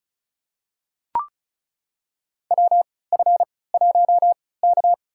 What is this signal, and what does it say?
Morse code, heard as a keyed tone, about 36 wpm, spaced as for 20 wpm, 700 Hz. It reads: WF1K
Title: USA Call Signs - 4 Characters - 3x Character Spacing - 35wpm